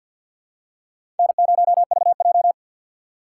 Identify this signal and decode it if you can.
Morse code, heard as a keyed tone, about 52 wpm, 700 Hz. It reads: N0VJ
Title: USA Call Signs - 2x Character Spacing - 50wpm